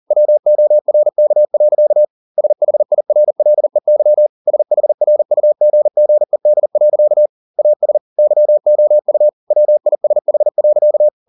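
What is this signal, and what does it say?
Morse code, heard as a keyed tone, about 40 wpm, 600 Hz. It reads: WORK. SHIRLEY SHRUGGED. AS YOU WISH.
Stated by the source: Morse code at 40 wpm, practice text, CW med Gnister.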